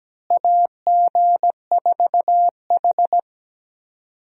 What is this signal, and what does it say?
Morse code, heard as a keyed tone, about 17 wpm, 700 Hz. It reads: AG4H
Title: USA Call Signs - 4 Characters - Repeated 2x - 17wpm